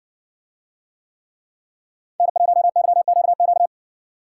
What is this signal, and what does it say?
Morse code, heard as a keyed tone, about 60 wpm, 700 Hz. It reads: N1YQQ